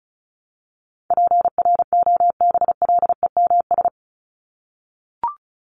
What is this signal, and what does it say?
Morse code, heard as a keyed tone, about 34 wpm, 700 Hz. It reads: PROBLEMS
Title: ES:S Suffix - Top 100 Words with Base Words 35wpm